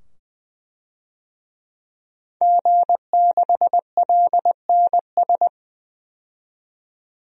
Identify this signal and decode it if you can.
Morse code, heard as a keyed tone, about 20 wpm, 700 Hz. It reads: G6LNS
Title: DXCC-100 Call Signs - 5 Characters - 20wpm